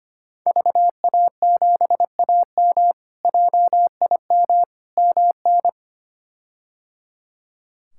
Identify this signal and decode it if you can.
Morse code, heard as a keyed tone, about 25 wpm, 700 Hz. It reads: VA7AM JIM MN
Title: K1USN SST QSO Practice - 25wpm